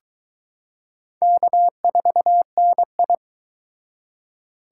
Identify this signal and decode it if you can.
Morse code, heard as a keyed tone, about 22 wpm, 700 Hz. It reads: K4NI